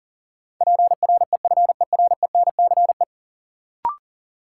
Morse code, heard as a keyed tone, about 38 wpm, 700 Hz. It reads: PREFERENCE
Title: Sentences from Top 1500 Words - New Word Drill - 40wpm